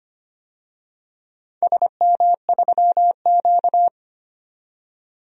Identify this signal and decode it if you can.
Morse code, heard as a keyed tone, about 25 wpm, 700 Hz. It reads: SM3Q
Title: DXCC-100 Call Signs - 4 Characters - 25wpm